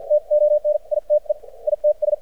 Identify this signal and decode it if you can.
Morse code, heard as a keyed tone, about 56 wpm, spaced as for 43 wpm, 600 Hz. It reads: TONITE ITS
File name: archived / CRSnet